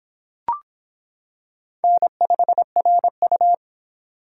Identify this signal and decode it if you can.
Morse code, heard as a keyed tone, about 26 wpm, 700 Hz. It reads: N5RU